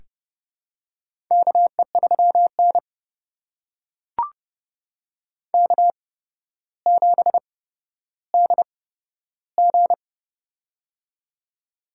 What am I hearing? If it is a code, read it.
Morse code, heard as a keyed tone, about 31 wpm, spaced as for 13 wpm, 700 Hz. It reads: KE3N K7DG
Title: USA Call Signs - 4 Characters - 8x Character Spacing - 30wpm